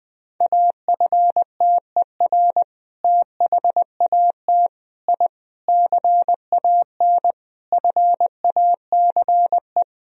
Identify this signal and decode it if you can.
Morse code, heard as a keyed tone, about 20 wpm, 700 Hz. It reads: AFTER THAT I CAN FACE
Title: Sentences from Top 1000 Words - Review Part 2 - 20wpm